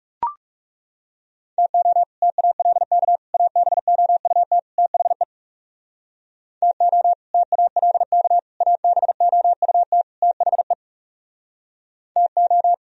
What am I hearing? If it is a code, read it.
Morse code, heard as a keyed tone, about 40 wpm, 700 Hz. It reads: TO TALK ABOUT THE TO TALK ABOUT THE TO